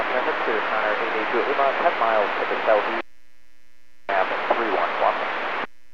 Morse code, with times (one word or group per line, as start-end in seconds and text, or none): none